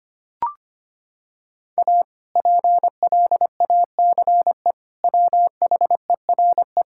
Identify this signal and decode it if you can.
Morse code, heard as a keyed tone, about 25 wpm, 700 Hz. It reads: A PLACE WHERE
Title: Sentences from Top 1000 Words - Review Part 5 - 25wpm